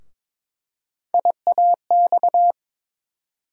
Morse code, heard as a keyed tone, about 22 wpm, 700 Hz. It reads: IAX
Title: USA Call Sign Postfixes - 3 Letters 22wpm